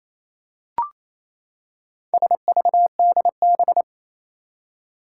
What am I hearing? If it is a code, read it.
Morse code, heard as a keyed tone, about 28 wpm, 700 Hz. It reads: SVDB
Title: Four Challenging Letter-Number Combinations 28wpm